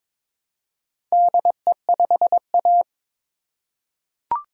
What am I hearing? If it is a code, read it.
Morse code, heard as a keyed tone, about 21 wpm, 700 Hz. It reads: DE5A
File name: practice / 22wpm / Four Letter-Number Combinations - 22wpm